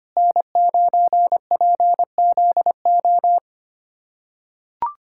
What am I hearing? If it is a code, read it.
Morse code, heard as a keyed tone, about 25 wpm, 700 Hz. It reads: N9PZO
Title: USA Callsign DE Callsign - 5 Characters - 25wpm